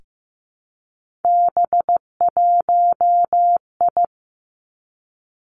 Morse code, heard as a keyed tone, about 15 wpm, 700 Hz. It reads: B1I
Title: Three Challenging Letter-Number Combinations 15wpm